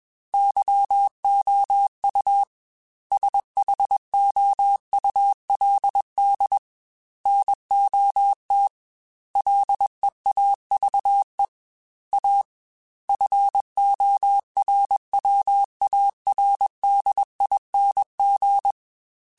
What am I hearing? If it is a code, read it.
Morse code, heard as a keyed tone, about 21 wpm, 800 Hz. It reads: YOU SHOULD NOT LEAVE A FORWARDING